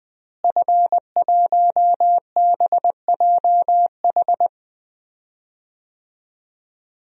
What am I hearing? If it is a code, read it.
Morse code, heard as a keyed tone, about 20 wpm, 700 Hz. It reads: F1BJH
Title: DXCC-100 Call Signs - 5 Characters - Repeated 2x - 20wpm